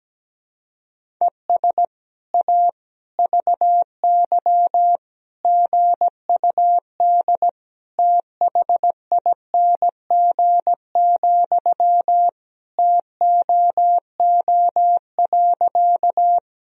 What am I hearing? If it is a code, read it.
Morse code, heard as a keyed tone, about 17 wpm, 700 Hz. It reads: ES A VY GUD THING, TOO.
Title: Sentences from Top 200 Words and Abbreviations 17wpm